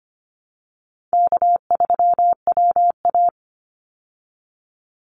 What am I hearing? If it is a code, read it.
Morse code, heard as a keyed tone, about 25 wpm, 700 Hz. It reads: K3WA
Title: Call Signs Repeated 2x - 25wpm